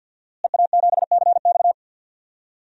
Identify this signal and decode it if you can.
Morse code, heard as a keyed tone, about 51 wpm, 700 Hz. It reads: EA7CX
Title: DXCC-100 Call Signs - 5 Characters - Repeated 2x - 50wpm